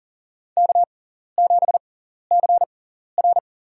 Morse code, heard as a keyed tone, about 40 wpm, spaced as for 14 wpm, 700 Hz. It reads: K7CR
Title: USA Call Signs - 4 Characters - 6x Character Spacing - 40wpm